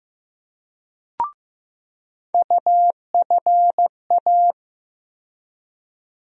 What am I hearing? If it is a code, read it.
Morse code, heard as a keyed tone, about 15 wpm, 700 Hz. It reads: UFA
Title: USA Call Sign Postfixes - 3 Letters 15wpm